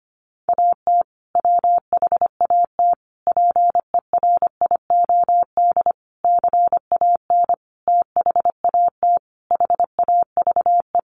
Morse code, heard as a keyed tone, about 25 wpm, 700 Hz. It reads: AT WHAT PERIOD CAN THAT HAVE
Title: Sentences from Top 800 Words 25wpm